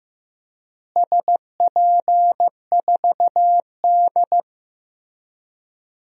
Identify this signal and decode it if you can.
Morse code, heard as a keyed tone, about 15 wpm, 700 Hz. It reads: SP4D
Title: DXCC-100 Call Signs - 4 Characters - Repeated 3x - 15wpm